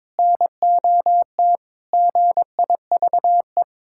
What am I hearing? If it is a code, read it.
Morse code, heard as a keyed tone, about 22 wpm, 700 Hz. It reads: NOT GIVE
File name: practice / 22wpm / Sentences with 3-Word Phrases - 22wpm